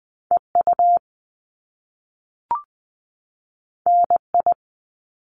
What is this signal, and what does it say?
Morse code, heard as a keyed tone, about 20 wpm, 700 Hz. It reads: EU NI
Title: Top Two Letter Combinations in Words 20wpm